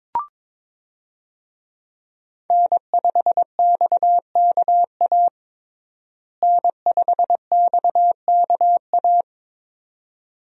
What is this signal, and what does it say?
Morse code, heard as a keyed tone, about 22 wpm, 700 Hz. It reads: N5XKA N5XKA
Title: USA Call Signs - 5 Characters - Repeated 2x - 22wpm